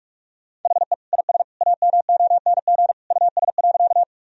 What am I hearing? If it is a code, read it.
Morse code, heard as a keyed tone, about 47 wpm, 700 Hz. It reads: HE IS AMONG US.
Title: Sentences from Top 1000 Words - Review Part 3 - 45wpm